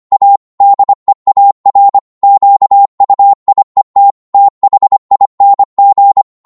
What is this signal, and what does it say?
Morse code, heard as a keyed tone, about 25 wpm, 800 Hz. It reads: A DEAR QUIET THING